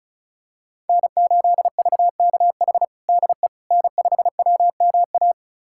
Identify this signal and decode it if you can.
Morse code, heard as a keyed tone, about 36 wpm, 700 Hz. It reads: N8VKH DE N5WMA
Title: USA Callsign DE Callsign - 5 Characters - 35wpm